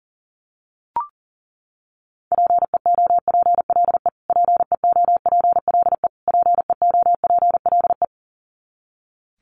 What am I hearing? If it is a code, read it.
Morse code, heard as a keyed tone, about 37 wpm, 700 Hz. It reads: PEOPLE PEOPLE PEOPLE
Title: Top 200 Words Repeated 40wpm